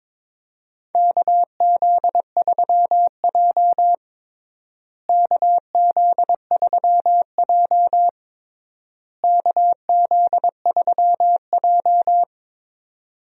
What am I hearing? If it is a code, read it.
Morse code, heard as a keyed tone, about 21 wpm, 700 Hz. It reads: KZ3J KZ3J KZ3J
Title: USA Call Signs - 4 Characters - Repeated 3x - 22wpm